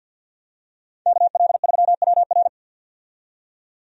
Morse code, heard as a keyed tone, about 51 wpm, 700 Hz. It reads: KL3WR
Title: USA Call Signs - 5 Characters - 50wpm